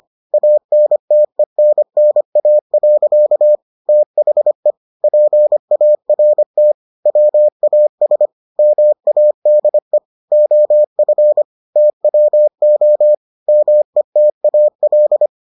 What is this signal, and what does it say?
Morse code, heard as a keyed tone, about 25 wpm, 600 Hz. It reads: ANTENNA. THE PART WAS MADE OF TWO METAL